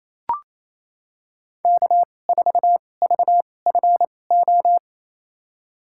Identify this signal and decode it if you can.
Morse code, heard as a keyed tone, about 28 wpm, spaced as for 20 wpm, 700 Hz. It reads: K4VFO
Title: USA Call Signs - 2x Character Spacing - 28wpm